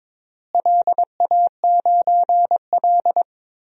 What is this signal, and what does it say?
Morse code, heard as a keyed tone, about 22 wpm, 700 Hz. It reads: LA9L
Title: DXCC-100 Call Signs - 4 Characters - Repeated 2x - 22wpm